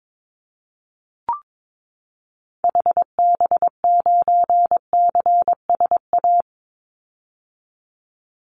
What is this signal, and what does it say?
Morse code, heard as a keyed tone, about 22 wpm, 700 Hz. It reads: HB9CSA
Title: Call Signs - 22wpm